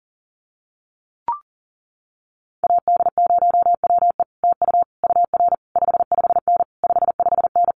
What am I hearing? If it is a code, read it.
Morse code, heard as a keyed tone, about 40 wpm, 700 Hz. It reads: AD0WE TU UR 55N 55N
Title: Common QSO Phrases Encore 40wpm